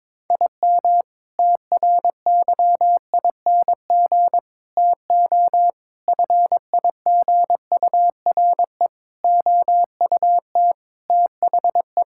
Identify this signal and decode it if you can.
Morse code, heard as a keyed tone, about 22 wpm, 700 Hz. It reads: IM TRYING TO FIGURE OUT THE
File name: practice / 22wpm / Sentences with 4-Word Phrases - 22wpm